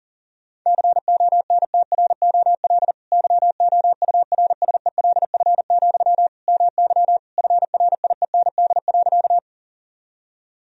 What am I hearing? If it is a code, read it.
Morse code, heard as a keyed tone, about 40 wpm, 700 Hz. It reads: CONTROL YOURSELF, MY FRIEND.
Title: Sentences from Top 1000 Words - Review Part 3 - 40wpm